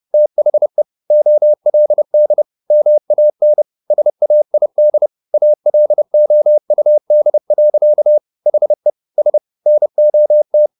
Morse code, heard as a keyed tone, about 30 wpm, 600 Hz. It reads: THE OLD MAN SAID ALOUD. HE S NOT
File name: the old man and the sea / the old man and the sea 1 30wpm